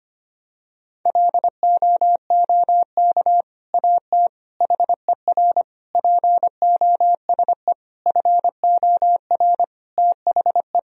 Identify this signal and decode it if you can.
Morse code, heard as a keyed tone, about 25 wpm, 700 Hz. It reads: LOOK AT HER POSE FOR THE
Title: Sentences from Top 1000 Words - Review Part 2 - 25wpm